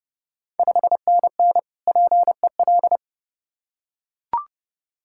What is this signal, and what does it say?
Morse code, heard as a keyed tone, about 32 wpm, 700 Hz. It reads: 5NN PEL
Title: Ontario QSO Party Exchange 30wpm